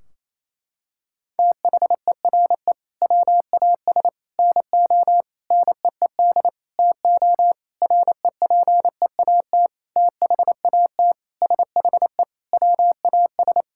Morse code, heard as a keyed tone, about 28 wpm, 700 Hz. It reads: THERE WAS NO NEED TO REPEAT THAT SHE WAS
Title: Sentences from Top 1600 Words 28wpm